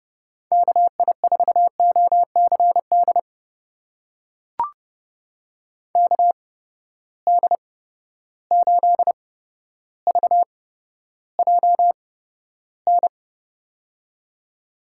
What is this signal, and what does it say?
Morse code, heard as a keyed tone, about 30 wpm, spaced as for 13 wpm, 700 Hz. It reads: KI4OCD KD8VJN
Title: USA Call Signs - 8x Character Spacing - 30wpm